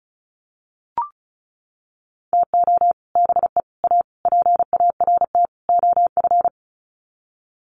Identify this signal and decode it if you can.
Morse code, heard as a keyed tone, about 34 wpm, 700 Hz. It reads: TO BE A PART OF
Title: Sets of 5 Words 35wpm